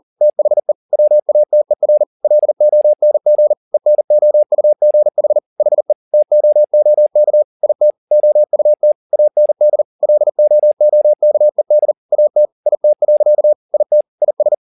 Morse code, heard as a keyed tone, about 40 wpm, 600 Hz. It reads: THE WATER LONG ENOUGH HE TOOK IT OUT AND LOOKED AT IT. IT IS
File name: the old man and the sea 2 40wpm